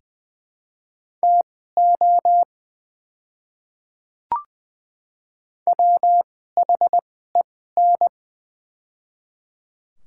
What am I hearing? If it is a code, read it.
Morse code, heard as a keyed tone, about 20 wpm, spaced as for 14 wpm, 700 Hz. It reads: TO WHEN
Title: 2x Character Spacing - Top 100 Words - 20wpm